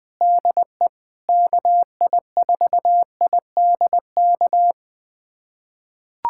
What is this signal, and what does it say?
Morse code, heard as a keyed tone, about 21 wpm, 700 Hz. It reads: DE KI4IDK
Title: USA Callsign DE Callsign - 6 Characters - 20wpm